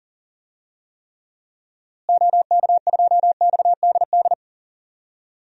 Morse code, heard as a keyed tone, about 37 wpm, 700 Hz. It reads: OK2XDD
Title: DXCC-100 Call Signs - 6 Characters - Repeated 2x - 40wpm